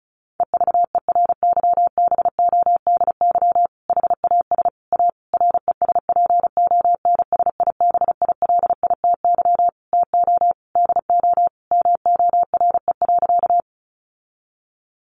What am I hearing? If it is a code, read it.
Morse code, heard as a keyed tone, about 34 wpm, 700 Hz. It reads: EVERYBODY HAS A RESPONSIBILITY TO DO MORE.
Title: Sentences from Top 1400 Words 35wpm